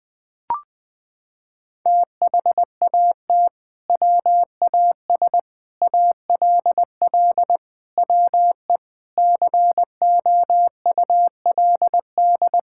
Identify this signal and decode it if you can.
Morse code, heard as a keyed tone, about 20 wpm, 700 Hz. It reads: THAT WAS ALL WE COULD
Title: Sentences from Top 1600 Words 20wpm